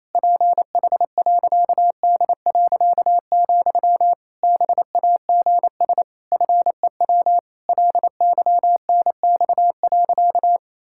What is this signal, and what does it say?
Morse code, heard as a keyed tone, about 28 wpm, 700 Hz. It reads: PH.D., BAGS FEW LYNX.
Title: Pangram Challenge v2 - 28wpm